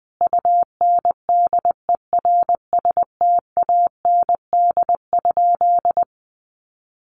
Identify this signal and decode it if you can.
Morse code, heard as a keyed tone, about 20 wpm, 700 Hz. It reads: UNDERSTAND?
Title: Sentences from Top 1500 Words 20wpm